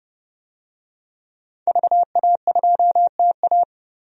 Morse code, heard as a keyed tone, about 31 wpm, 700 Hz. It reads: VA2TA
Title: DXCC-100 Call Signs - 5 Characters - 30wpm